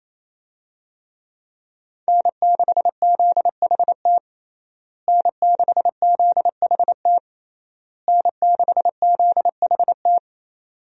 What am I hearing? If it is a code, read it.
Morse code, heard as a keyed tone, about 27 wpm, 700 Hz. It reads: N6ZHT N6ZHT N6ZHT
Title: USA Call Signs - 5 Characters - Repeated 3x - 28wpm